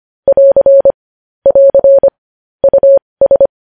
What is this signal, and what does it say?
Morse code, heard as a keyed tone, about 25 wpm, spaced as for 20 wpm, 550 Hz. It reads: <AR> <AR> US
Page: news-headlines-in-morse-code-at-20-wpm